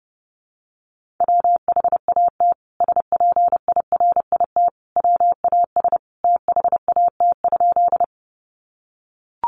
Morse code, heard as a keyed tone, about 30 wpm, 700 Hz. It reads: WHAT SPIRIT WAS THAT?
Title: Sentences from Top 1400 Words 30wpm